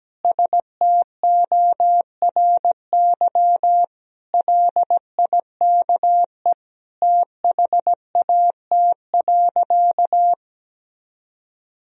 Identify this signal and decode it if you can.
Morse code, heard as a keyed tone, about 17 wpm, 700 Hz. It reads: STORY LIKE THAT.
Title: Sentences from Top 1000 Words - Review Part 3 - 17wpm